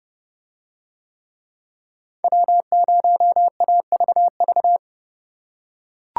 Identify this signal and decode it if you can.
Morse code, heard as a keyed tone, about 29 wpm, 700 Hz. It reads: W0AVV